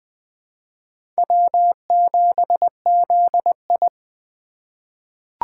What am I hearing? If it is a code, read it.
Morse code, heard as a keyed tone, about 20 wpm, 700 Hz. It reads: W7ZI
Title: USA Call Signs - 4 Characters - 3x Character Spacing - 20wpm